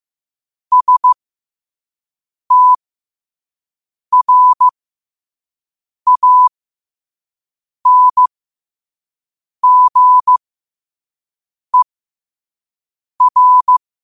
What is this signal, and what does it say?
Morse code, heard as a keyed tone, about 15 wpm, spaced as for 5 wpm, 1000 Hz. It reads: STRANGER
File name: qotdfiles / 5wpm